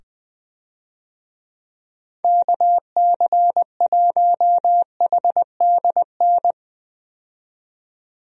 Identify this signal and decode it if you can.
Morse code, heard as a keyed tone, about 20 wpm, 700 Hz. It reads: KC1HDN
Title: USA Call Signs - 6 Characters - 20wpm